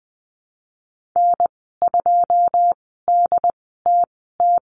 Morse code, heard as a keyed tone, about 20 wpm, spaced as for 15 wpm, 700 Hz. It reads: N2DTT
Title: 122 Lesson 20g - Number 3 - Call Signs 20wpm